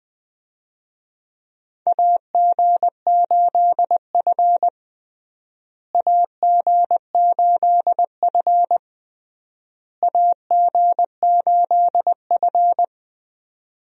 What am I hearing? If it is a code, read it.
Morse code, heard as a keyed tone, about 20 wpm, 700 Hz. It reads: AG8F AG8F AG8F
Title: USA Call Signs - 4 Characters - Repeated 3x - 20wpm